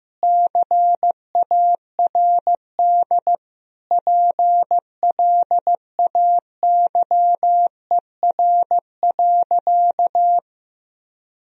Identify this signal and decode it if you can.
Morse code, heard as a keyed tone, about 15 wpm, 700 Hz. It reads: CARD PLAYER.
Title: Sentences from Top 1000 Words - Review Part 4 - 15wpm